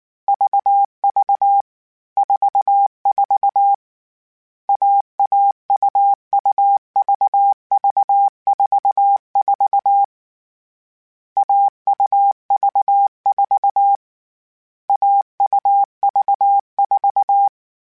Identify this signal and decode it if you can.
Morse code, heard as a keyed tone, about 19 wpm, 800 Hz. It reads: VV 44 AAUUVV44 AUV4 AUV4
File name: Difficult Combos / Difficult Combos-mp3s / AUV AUV4 @18